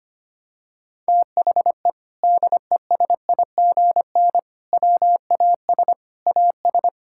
Morse code, heard as a keyed tone, about 25 wpm, 700 Hz. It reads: THE DESIGN WAS AS
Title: Sentences from Top 1000 Words - Review Part 5 - 25wpm